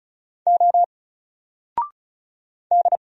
Morse code, heard as a keyed tone, about 36 wpm, 700 Hz. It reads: O D
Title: Single Letter-Number Alt 35wpm